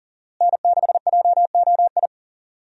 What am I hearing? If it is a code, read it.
Morse code, heard as a keyed tone, about 40 wpm, 700 Hz. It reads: N6JOI
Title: USA Call Signs - 5 Characters - Repeated 2x - 40wpm